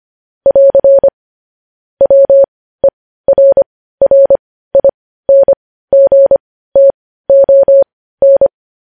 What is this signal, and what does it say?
Morse code, heard as a keyed tone, about 25 wpm, spaced as for 15 wpm, 550 Hz. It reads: <AR> WERRINGTON
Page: news-headlines-in-morse-code-at-15-wpm